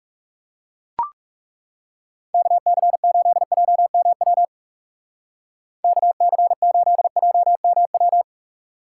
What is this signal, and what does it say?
Morse code, heard as a keyed tone, about 39 wpm, 700 Hz. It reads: KC8JMW KC8JMW